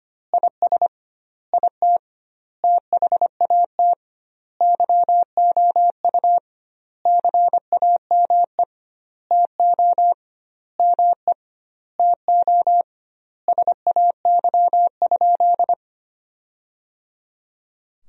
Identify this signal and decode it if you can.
Morse code, heard as a keyed tone, about 25 wpm, 700 Hz. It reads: IS IT THAT YOU CAME TO ME TO SAY?
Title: 2x Word-Spacing - Sentences from Top 200 Words - 25wpm